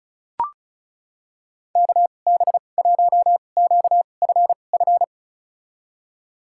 Morse code, heard as a keyed tone, about 34 wpm, spaced as for 25 wpm, 690 Hz. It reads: KB1QFF
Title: USA Call Signs - 6 Characters - 2x Character Spacing - 35wpm